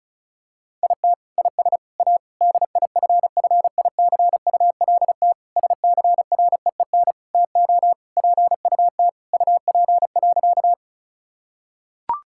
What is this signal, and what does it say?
Morse code, heard as a keyed tone, about 36 wpm, 700 Hz. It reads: IT IS A DIFFICULT SCREEN TO PUT UP.
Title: Sentences from Top 1000 Words - Review Part 6 - 35wpm